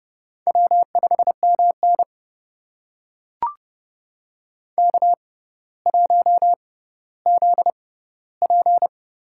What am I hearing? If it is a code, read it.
Morse code, heard as a keyed tone, about 30 wpm, spaced as for 15 wpm, 700 Hz. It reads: W5MN K1ZP
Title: USA Call Signs - 4 Characters - 6x Character Spacing - 30wpm